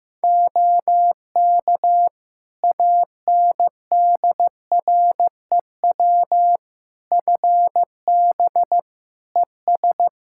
Morse code, heard as a keyed tone, about 15 wpm, 700 Hz. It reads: OK ANDREW FB ES
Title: Practice QSO - Exchange 2 - Info - 15wpm